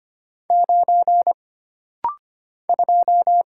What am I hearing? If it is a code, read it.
Morse code, heard as a keyed tone, about 25 wpm, 700 Hz. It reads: 9 2
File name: Single Character (Letter:Number:Punctuation:Prosign) - Rapid-Fire - 25wpm